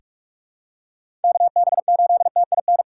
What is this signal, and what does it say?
Morse code, heard as a keyed tone, about 44 wpm, 700 Hz. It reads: KB8TIN
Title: USA Call Signs - 6 Characters - 6x Character Spacing - 45wpm